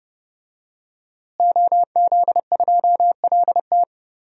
Morse code, heard as a keyed tone, about 30 wpm, 700 Hz. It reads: OZ2LT
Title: DXCC-100 Call Signs - 5 Characters - Repeated 2x - 30wpm